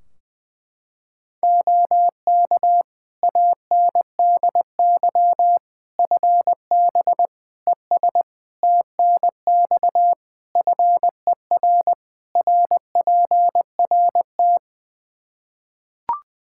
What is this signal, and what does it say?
Morse code, heard as a keyed tone, about 20 wpm, 700 Hz. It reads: OK ANDY FB ES TNX FER RPRT
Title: K1USN SST QSO Practice 20wpm